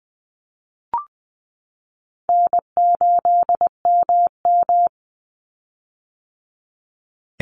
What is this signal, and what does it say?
Morse code, heard as a keyed tone, about 20 wpm, 700 Hz. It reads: N8MM
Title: USA Call Signs 20wpm